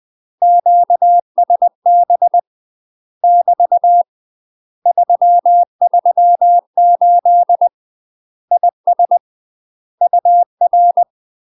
Morse code, heard as a keyed tone, about 20 wpm, 700 Hz. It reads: QSB <BT> 338 IS UR
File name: UK MASTER QSOs TRAINING / 20WPM / UK MASTER QSOs 0050